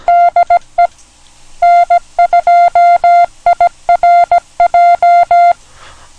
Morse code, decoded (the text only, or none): DE N2IRJ